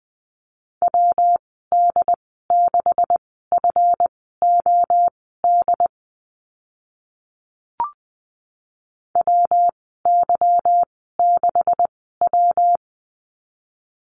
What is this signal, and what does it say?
Morse code, heard as a keyed tone, about 20 wpm, spaced as for 15 wpm, 700 Hz. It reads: WD6FOD WY6W
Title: 167 Lesson 27g - Number 7 - Call Signs 20wpm